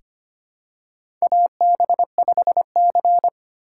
Morse code, heard as a keyed tone, about 25 wpm, 700 Hz. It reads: AB5C
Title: K1USN SST QSO Practice - 25wpm